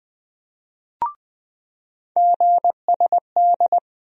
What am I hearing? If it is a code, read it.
Morse code, heard as a keyed tone, about 21 wpm, 700 Hz. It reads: GSD
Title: USA Call Sign Postfixes - 3 Letters 20wpm